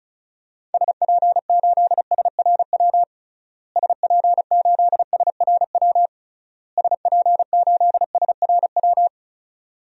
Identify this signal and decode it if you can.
Morse code, heard as a keyed tone, about 34 wpm, 700 Hz. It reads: SP8SRW SP8SRW SP8SRW
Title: DXCC-100 Call Signs - 6 Characters - Repeated 3x - 35wpm